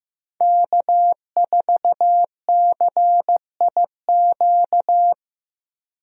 Morse code, heard as a keyed tone, about 15 wpm, 700 Hz. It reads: K4CIQ
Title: USA Call Signs - 5 Characters - Repeated 3x - 15wpm